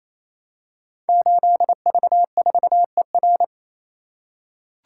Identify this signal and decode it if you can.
Morse code, heard as a keyed tone, about 28 wpm, 700 Hz. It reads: 8V4ER
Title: Code Groups - 5 Letter:Numbers - 28wpm